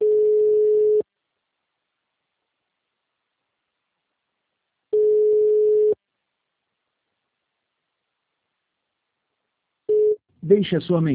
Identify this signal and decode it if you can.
Ringback tone (Europe, 425 Hz), with speech.
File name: out- -2016- - - .12